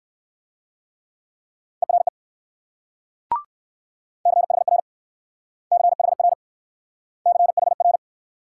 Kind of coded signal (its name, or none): Morse code